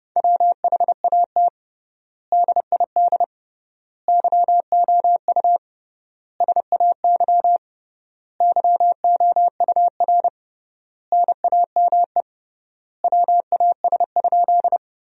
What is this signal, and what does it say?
Morse code, heard as a keyed tone, about 32 wpm, 700 Hz. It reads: WHAT DID YOU SAY YOUR NAME WAS?
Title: 3x Word-Spacing - Sentences from Top 200 Words - 30wpm